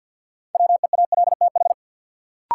Morse code, heard as a keyed tone, about 53 wpm, 700 Hz. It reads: WEALTH
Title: Sentences from Top 1000 Words - Review Part 4 - New Word Drill - 50wpm